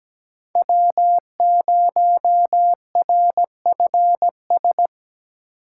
Morse code, heard as a keyed tone, about 17 wpm, 700 Hz. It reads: W0RFS
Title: USA Call Signs - 5 Characters - 2x Character Spacing - 17wpm